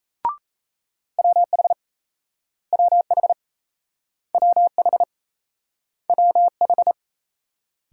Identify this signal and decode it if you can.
Morse code, heard as a keyed tone, about 38 wpm, 700 Hz. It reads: WH WH WH WH